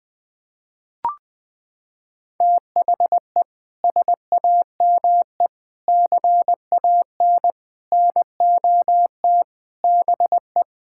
Morse code, heard as a keyed tone, about 20 wpm, 700 Hz. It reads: THE SAME CAN NOT BE
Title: Sets of 6 Words - 20wpm